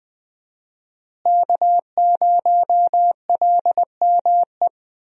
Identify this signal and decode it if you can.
Morse code, heard as a keyed tone, about 20 wpm, 700 Hz. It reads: K0LME
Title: USA Call Signs - 5 Characters - Repeated 2x - 20wpm